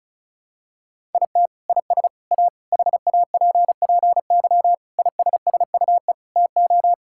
Morse code, heard as a keyed tone, about 34 wpm, 700 Hz. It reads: IT IS A HAPPY ISSUE TO
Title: Sentences from Top 1000 Words - Review Part 6 - 35wpm